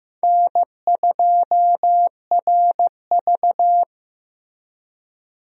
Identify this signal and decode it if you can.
Morse code, heard as a keyed tone, about 15 wpm, 700 Hz. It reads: N2RV
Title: USA Call Signs - 4 Characters - Repeated 3x - 15wpm